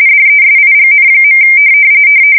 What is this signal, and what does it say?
RTTY, about 45.45 baud. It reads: DESIGN.
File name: fsk170 / battleshipnj